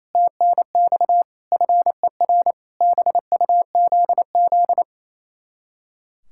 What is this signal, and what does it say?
Morse code, heard as a keyed tone, about 34 wpm, 700 Hz. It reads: TNX FER BUZZ
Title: Speed-Racing Common QSO Phrases - 28wpm